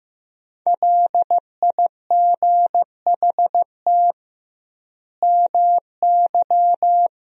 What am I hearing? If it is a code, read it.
Morse code, heard as a keyed tone, about 16 wpm, 700 Hz. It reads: LIGHT MY
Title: 2x Word-Spacing - Sentences from Top 400 Words - 15wpm